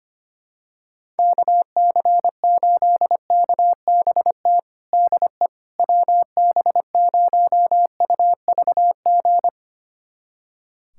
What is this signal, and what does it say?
Morse code, heard as a keyed tone, about 25 wpm, 700 Hz. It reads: KC8KBT DE WB0UVG
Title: USA Callsign DE Callsign - 6 Characters - 25wpm